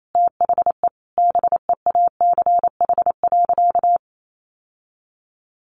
Morse code, heard as a keyed tone, about 28 wpm, 700 Hz. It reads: THE BEACH.
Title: Sentences from Top 1500 Words 28wpm